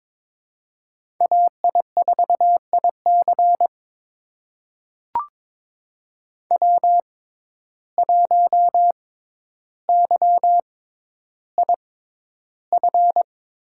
Morse code, heard as a keyed tone, about 22 wpm, spaced as for 11 wpm, 700 Hz. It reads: AI4IC W1YIF
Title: USA Call Signs - 5 Characters - 6x Character Spacing - 22wpm